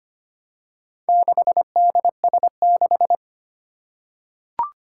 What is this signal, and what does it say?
Morse code, heard as a keyed tone, about 25 wpm, 700 Hz. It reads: 6DS6
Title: Four Challenging Letter-Number Combinations 25wpm